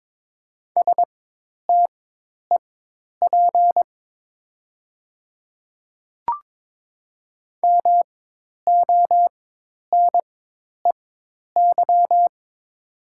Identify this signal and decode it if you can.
Morse code, heard as a keyed tone, about 22 wpm, spaced as for 10 wpm, 700 Hz. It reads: STEP MONEY
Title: 4x Character Spacing - Top 500 Words - 22wpm